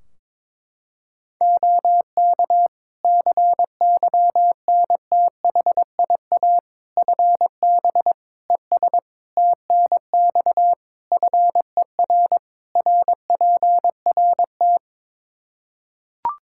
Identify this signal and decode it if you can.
Morse code, heard as a keyed tone, about 22 wpm, 700 Hz. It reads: OK CYNTHIA FB ES TNX FER RPRT
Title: K1USN SST QSO Practice 22wpm